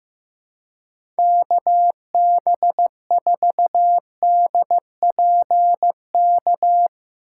Morse code, heard as a keyed tone, about 15 wpm, 700 Hz. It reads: KB4DPK
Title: USA Call Signs - 6 Characters - 3x Character Spacing - 15wpm